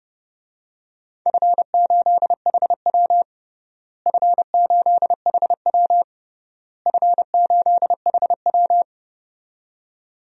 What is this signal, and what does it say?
Morse code, heard as a keyed tone, about 31 wpm, 700 Hz. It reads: F8HW F8HW F8HW
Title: DXCC-100 Call Signs - 4 Characters - Repeated 3x - 30wpm